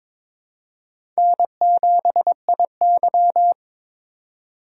Morse code, heard as a keyed tone, about 22 wpm, 700 Hz. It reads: N7IY
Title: USA Call Signs - 4x Character Spacing - 22wpm